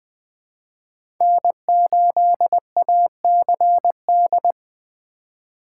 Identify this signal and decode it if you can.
Morse code, heard as a keyed tone, about 20 wpm, 700 Hz. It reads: N8ACD
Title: USA Call Signs - 5 Characters - 4x Character Spacing - 20wpm